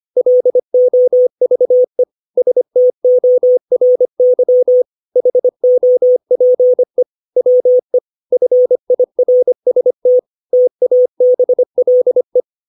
Morse code, heard as a keyed tone, about 25 wpm, 500 Hz. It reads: LOVE STORY HOPE WE FIRST TABLE
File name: 632 words / 632 Words 25WPM